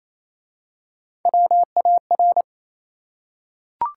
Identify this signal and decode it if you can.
Morse code, heard as a keyed tone, about 27 wpm, 690 Hz. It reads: WAR
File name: Easy Words - Speed-Racing with Hints - 22wpm